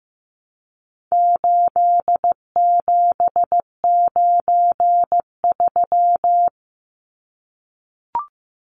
Morse code, heard as a keyed tone, about 15 wpm, 700 Hz. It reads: 8793
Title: Numbers - 4-Digit 15wpm